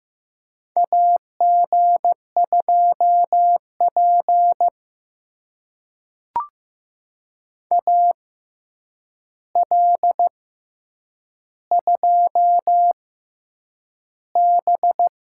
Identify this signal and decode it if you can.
Morse code, heard as a keyed tone, about 15 wpm, spaced as for 8 wpm, 700 Hz. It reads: AG2P AL2B
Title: USA Call Signs - 4 Characters - 6x Character Spacing - 15wpm